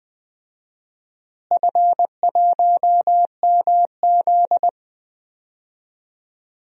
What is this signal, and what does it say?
Morse code, heard as a keyed tone, about 20 wpm, 700 Hz. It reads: F1MZ
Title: DXCC-100 Call Signs - 4 Characters - Repeated 3x - 20wpm